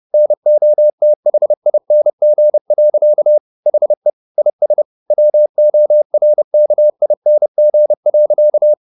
Morse code, heard as a keyed tone, about 30 wpm, 600 Hz. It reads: NOTHING. HE IS WORKING.
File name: the old man and the sea / the old man and the sea 1 30wpm